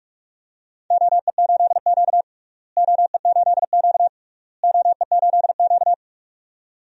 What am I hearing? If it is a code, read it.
Morse code, heard as a keyed tone, about 46 wpm, 700 Hz. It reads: OE8Q OE8Q OE8Q
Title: DXCC-100 Call Signs - 4 Characters - Repeated 3x - 45wpm